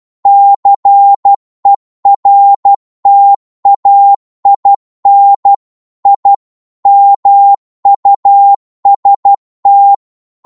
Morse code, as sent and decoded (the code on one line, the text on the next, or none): -.-. .-. - .- .. -. / .. / -- ..- ... -
CERTAIN I MUST